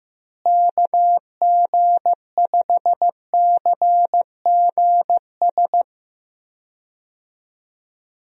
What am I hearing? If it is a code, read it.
Morse code, heard as a keyed tone, about 15 wpm, 700 Hz. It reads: KG5CGS
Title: USA Call Signs - 6 Characters - Repeated 3x - 15wpm